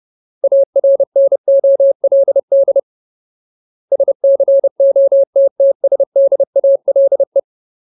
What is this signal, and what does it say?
Morse code, heard as a keyed tone, about 30 wpm, 550 Hz. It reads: ARNOLD SCOTTSDALE
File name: Ultimate Word List / 30WPM Ultimate